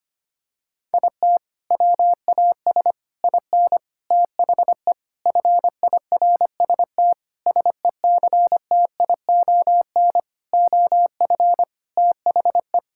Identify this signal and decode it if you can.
Morse code, heard as a keyed tone, about 25 wpm, 700 Hz. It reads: IT WAS IN THE FIRST SECTION OF THE
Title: Sentences from Top 1000 Words - Review Part 5 - 25wpm